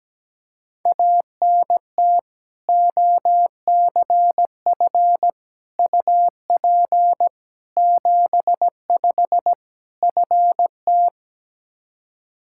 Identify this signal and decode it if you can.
Morse code, heard as a keyed tone, about 17 wpm, 700 Hz. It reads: ANT OCF UP 75 FT
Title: Practice QSO - Exchange 2 - Info - 17wpm